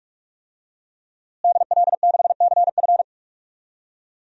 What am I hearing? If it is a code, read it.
Morse code, heard as a keyed tone, about 46 wpm, 700 Hz. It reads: DL6CF
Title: DXCC-100 Call Signs - 5 Characters - Repeated 2x - 45wpm